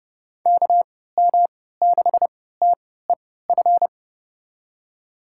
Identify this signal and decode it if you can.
Morse code, heard as a keyed tone, about 31 wpm, spaced as for 17 wpm, 700 Hz. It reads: KM6TEF
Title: USA Call Signs - 6 Characters - 3x Character Spacing - 30wpm